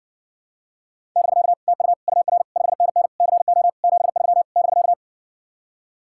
Morse code, heard as a keyed tone, about 61 wpm, 700 Hz. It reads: <BK> TU UR 5NN CO 73 <BK>